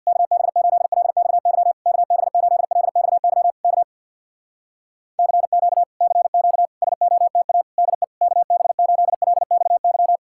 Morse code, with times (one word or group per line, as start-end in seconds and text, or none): 0.1-1.7 KB8LXY
1.9-3.5 KB8LXY
3.6-3.8 K
5.2-5.8 CQ
6.0-6.7 CQ
6.8-7.6 SOTA
7.8-8.0 DE
8.2-10.2 KB8LXY